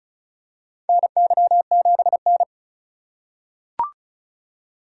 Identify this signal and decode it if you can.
Morse code, heard as a keyed tone, about 36 wpm, 700 Hz. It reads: NY7N
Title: K1USN SST QSO Practice - 35wpm